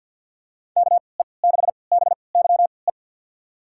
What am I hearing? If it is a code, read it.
Morse code, heard as a keyed tone, about 47 wpm, spaced as for 28 wpm, 700 Hz. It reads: KE6BYE
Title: USA Call Signs - 6 Characters - 3x Character Spacing - 50wpm